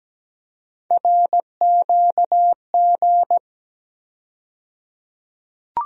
Morse code, heard as a keyed tone, about 17 wpm, 700 Hz. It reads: RQG
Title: USA Call Sign Postfixes - 3 Letters 17wpm